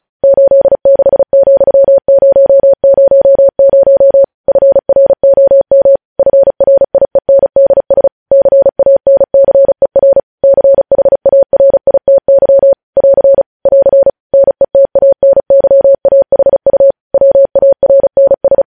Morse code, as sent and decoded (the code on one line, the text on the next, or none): ---.. -.... --..-- ----- ----- ----- / ..-. .-. --- -- / ..-. .-. .. -. -.. ... / -.-. .- -. -.-. .-. / -.-. .... .- .-. .. - -.-- / .-.-. / .-.-. / -. - .- -. -.-- .- .... ..- / .-- .- .-. -. ...
86,000 FROM FRIENDS CANCER CHARITY <AR> <AR> NETANYAHU WARNS